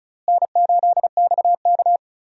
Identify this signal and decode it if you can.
Morse code, heard as a keyed tone, about 34 wpm, 700 Hz. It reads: N8XK
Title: USA Call Signs - 4 Characters - Repeated 3x - 35wpm